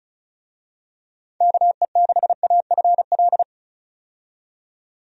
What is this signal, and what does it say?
Morse code, heard as a keyed tone, about 34 wpm, 700 Hz. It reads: KE6AFL
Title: USA Call Signs - 6 Characters - 35wpm